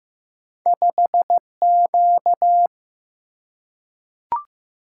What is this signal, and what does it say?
Morse code, heard as a keyed tone, about 15 wpm, 700 Hz. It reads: 5Q
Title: Single-Letter Single-Number 15wpm